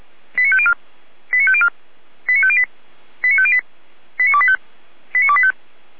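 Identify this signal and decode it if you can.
Five-tone selective call (ZVEI), heard as quick runs of five tones, tones of about 70 ms. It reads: 89504 89504 89508 89508 89285 89285
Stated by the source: ZVEI five-tone call received by SDR, logged on Feuerwehr Orlamünde audio server.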